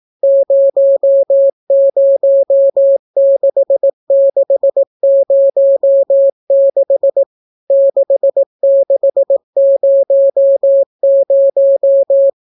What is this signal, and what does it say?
Morse code, heard as a keyed tone, about 18 wpm, 550 Hz. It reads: 006606 6600